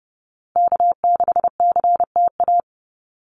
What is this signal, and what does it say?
Morse code, heard as a keyed tone, about 30 wpm, 700 Hz. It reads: K6CTA
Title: Call Signs Repeated 2x - 30wpm